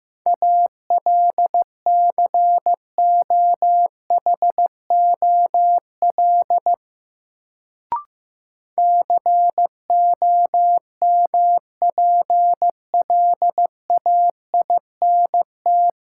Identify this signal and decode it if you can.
Morse code, heard as a keyed tone, about 15 wpm, 700 Hz. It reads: ALCOHOL COMPLAINT
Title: Sentences from Top 1400 Words - New Word Drill - 15wpm